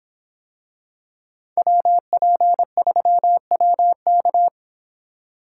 Morse code, heard as a keyed tone, about 26 wpm, 700 Hz. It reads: WP3WK